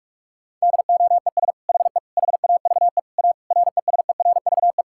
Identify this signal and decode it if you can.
Morse code, heard as a keyed tone, about 46 wpm, 700 Hz. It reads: DOES HE HAVE A RESERVE
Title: Sentences from Top 1000 Words - Review Part 6 - 45wpm